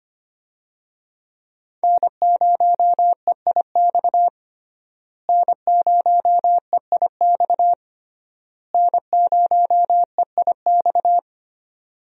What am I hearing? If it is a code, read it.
Morse code, heard as a keyed tone, about 25 wpm, 700 Hz. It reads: N0EIX N0EIX N0EIX
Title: USA Call Signs - 5 Characters - Repeated 3x - 25wpm